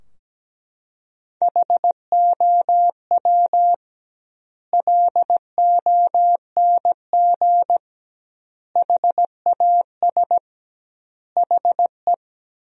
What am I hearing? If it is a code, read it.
Morse code, heard as a keyed tone, about 17 wpm, 700 Hz. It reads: HOW LONG HAS HE
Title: 2x Word-Spacing - Sentences from Top 100 Words - 17wpm